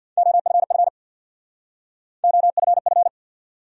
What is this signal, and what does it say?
Morse code, heard as a keyed tone, about 65 wpm, 700 Hz. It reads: OFF OFF